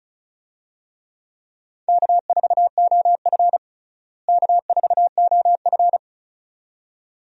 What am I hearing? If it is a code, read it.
Morse code, heard as a keyed tone, about 36 wpm, 700 Hz. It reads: K4OF K4OF
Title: USA Call Signs - 4 Characters - Repeated 2x - 35wpm